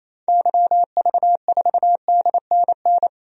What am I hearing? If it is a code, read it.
Morse code, heard as a keyed tone, about 28 wpm, 700 Hz. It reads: YV4DNN
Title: DXCC-100 Call Signs - 6 Characters - 28wpm